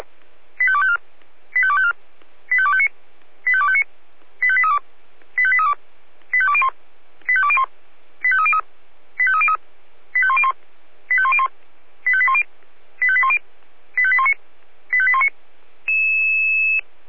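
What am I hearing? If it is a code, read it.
Five-tone selective call (ZVEI), heard as quick runs of five tones, tones of about 70 ms. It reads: 85254 85254 85259 85259 85832 85832 85201 85201 85303 85303 85101 85101 85810 85810 85819 85819
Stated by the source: ZVEI five-tone call received on SDR.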